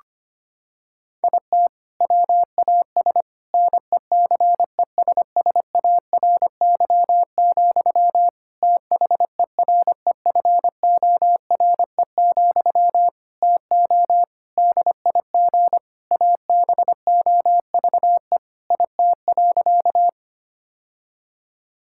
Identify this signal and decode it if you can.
Morse code, heard as a keyed tone, about 25 wpm, 700 Hz. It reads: IT WAS NECESSARY, THEREFORE, TO DIG ABOVE IT.
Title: Sentences from Top 1600 Words 25wpm